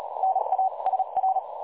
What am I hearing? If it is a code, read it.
Morse code, heard as a keyed tone, about 37 wpm, 760 Hz, with noise. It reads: NA SR
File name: dlazaawansowanych / nagraniazpasma / websdr recording 3565.0kHz